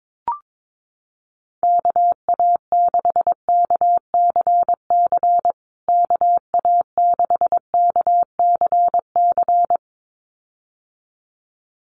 Morse code, heard as a keyed tone, about 22 wpm, 700 Hz. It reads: KA6KCC KA6KCC
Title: USA Call Signs Repeated 2x 22wpm